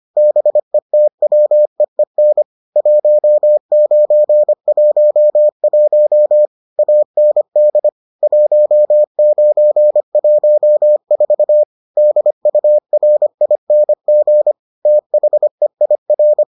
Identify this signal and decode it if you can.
Morse code, heard as a keyed tone, about 25 wpm, 600 Hz. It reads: BETWEEN 1911 AND 1914 DURING THEIR